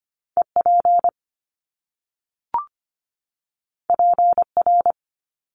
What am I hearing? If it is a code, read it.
Morse code, heard as a keyed tone, about 25 wpm, 700 Hz. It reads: EP PR
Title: Top Two Letter Combinations in Words 25wpm